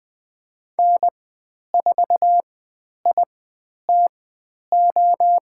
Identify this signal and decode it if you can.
Morse code, heard as a keyed tone, about 20 wpm, spaced as for 10 wpm, 700 Hz. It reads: N4ITO
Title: 109 Lesson 18g - Letter m - Call Signs 20wpm